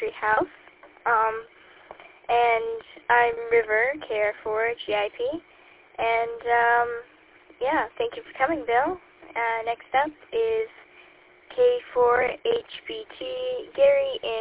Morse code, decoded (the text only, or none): none